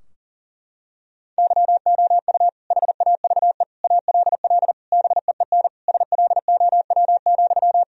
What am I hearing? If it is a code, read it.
Morse code, heard as a keyed tone, about 40 wpm, 700 Hz. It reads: YOU HAVE ALL BEEN SLOW,
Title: Sentences from Top 1000 Words - Review Part 2 - 40wpm